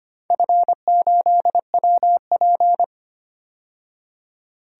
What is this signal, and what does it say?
Morse code, heard as a keyed tone, about 25 wpm, 700 Hz. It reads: F8WP
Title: DXCC-100 Call Signs - 4 Characters - 25wpm